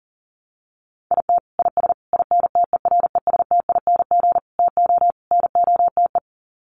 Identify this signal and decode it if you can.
Morse code, heard as a keyed tone, about 40 wpm, 700 Hz. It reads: IT IS INTERESTING TO NOTE
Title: Sets of 5 Words Encore2 40wpm